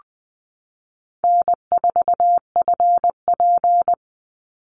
Morse code, heard as a keyed tone, about 20 wpm, 700 Hz. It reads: N4FP
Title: Call Signs Repeated 2x - 20wpm